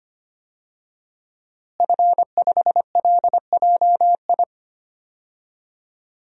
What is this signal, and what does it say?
Morse code, heard as a keyed tone, about 25 wpm, 700 Hz. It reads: F5LJI
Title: DXCC-100 Call Signs - 5 Characters - Repeated 3x - 25wpm